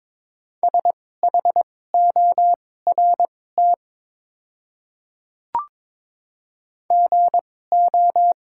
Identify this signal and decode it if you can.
Morse code, heard as a keyed tone, about 22 wpm, spaced as for 16 wpm, 700 Hz. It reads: SHORT GO
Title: 2x Character Spacing - Top 500 Words - 22wpm